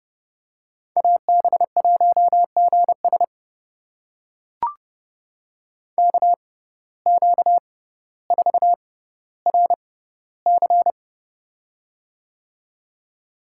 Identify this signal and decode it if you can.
Morse code, heard as a keyed tone, about 32 wpm, spaced as for 16 wpm, 700 Hz. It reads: AB1GS KQ4RC
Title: USA Call Signs - 5 Characters - 6x Character Spacing - 30wpm